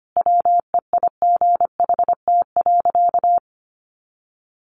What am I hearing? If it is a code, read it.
Morse code, heard as a keyed tone, about 25 wpm, 700 Hz. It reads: WEIGHT.
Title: Sentences from Top 800 Words 25wpm